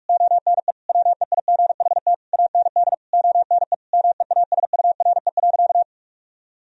Morse code, heard as a keyed tone, about 46 wpm, 700 Hz. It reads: ONE WEIGHT AND ONE MEASURE.
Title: Sentences from Top 1000 Words - Review Part 3 - 45wpm